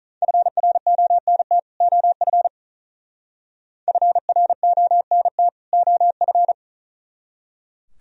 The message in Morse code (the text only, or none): FRONT OF FRONT OF